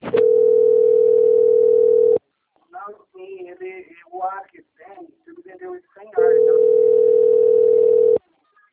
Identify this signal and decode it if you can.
Ringback tone (North America), 440 + 480 Hz, with speech.
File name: out- - - - - .769